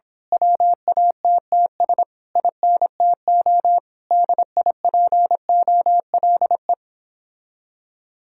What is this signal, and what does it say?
Morse code, heard as a keyed tone, about 26 wpm, 700 Hz. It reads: WATTS INTO DIPOLE
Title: Speed-Racing Common QSO Phrases - 22wpm